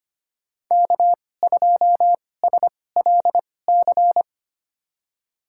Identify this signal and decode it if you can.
Morse code, heard as a keyed tone, about 25 wpm, spaced as for 18 wpm, 700 Hz. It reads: K2SLC
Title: USA Call Signs - 5 Characters - 2x Character Spacing - 25wpm